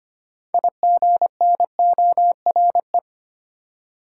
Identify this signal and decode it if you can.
Morse code, heard as a keyed tone, about 25 wpm, 700 Hz. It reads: IGNORE
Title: Six Letter Words - 25wpm